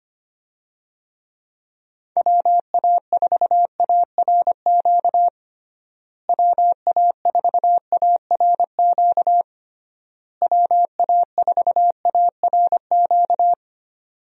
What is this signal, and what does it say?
Morse code, heard as a keyed tone, about 25 wpm, 700 Hz. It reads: WA4ARQ WA4ARQ WA4ARQ
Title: USA Call Signs - 6 Characters - Repeated 3x - 25wpm